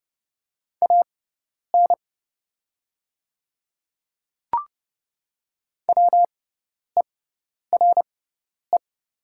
Morse code, heard as a keyed tone, about 30 wpm, spaced as for 10 wpm, 700 Hz. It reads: AN WERE
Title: 6x Character Spacing - Top 100 Words - 30wpm